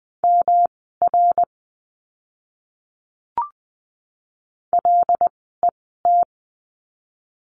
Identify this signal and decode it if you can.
Morse code, heard as a keyed tone, about 20 wpm, spaced as for 15 wpm, 700 Hz. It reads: MR LET
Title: 244 Lesson 39f - Letter z - Words 20wpm